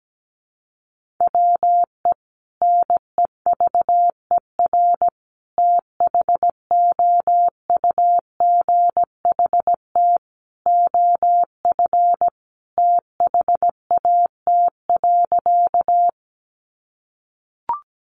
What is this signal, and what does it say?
Morse code, heard as a keyed tone, about 17 wpm, 700 Hz. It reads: WE NEVER THOUGHT OF THAT.
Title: Sentences from Top 300 Words 17wpm